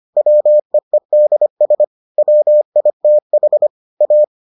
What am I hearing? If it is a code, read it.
Morse code, heard as a keyed tone, about 25 wpm, 600 Hz. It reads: WEEDS WITH A